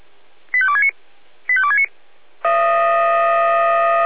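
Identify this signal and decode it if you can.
Five-tone selective call (ZVEI), heard as quick runs of five tones, tones of about 70 ms. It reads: 85269 85269
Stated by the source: Five-tone ZVEI selective call 85330, received on an SDR.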